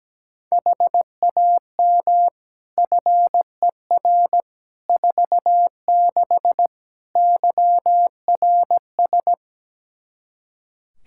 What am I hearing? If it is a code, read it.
Morse code, heard as a keyed tone, about 17 wpm, 700 Hz. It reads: HAM FER 46 YRS
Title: Practice QSO - Exchange 3 - More Info - 17wpm